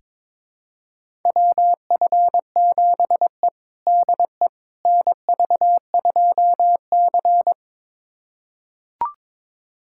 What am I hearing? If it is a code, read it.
Morse code, heard as a keyed tone, about 22 wpm, 700 Hz. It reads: WF7E DE NV2C
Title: USA Callsign DE Callsign - 4 Characters - 22wpm